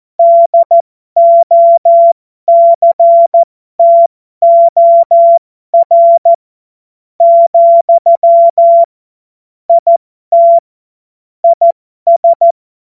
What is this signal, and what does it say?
Morse code, heard as a keyed tone, about 14 wpm, spaced as for 11 wpm, 680 Hz. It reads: DOCTOR, IT IS
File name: Morsekurs / Witze